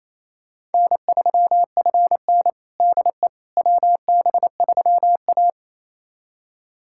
Morse code, heard as a keyed tone, about 28 wpm, 700 Hz. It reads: N3FN DE WB3A
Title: USA Callsign DE Callsign - 4 Characters - 28wpm